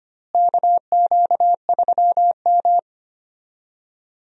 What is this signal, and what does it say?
Morse code, heard as a keyed tone, about 25 wpm, 700 Hz. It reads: KQ3M
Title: USA Call Signs - 4 Characters - Repeated 3x - 25wpm